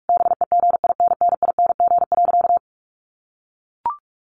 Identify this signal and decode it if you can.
Morse code, heard as a keyed tone, about 44 wpm, 700 Hz. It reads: BEGINNING.
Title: Sentences from Top 1500 Words 45wpm